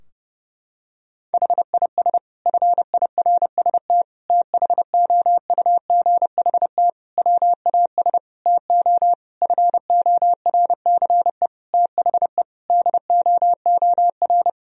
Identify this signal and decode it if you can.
Morse code, heard as a keyed tone, about 30 wpm, 700 Hz. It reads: HIS FIRST THOUGHT WAS TO FORCE THE DOOR
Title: Sentences from Top 1000 Words - Review Part 3 - 30wpm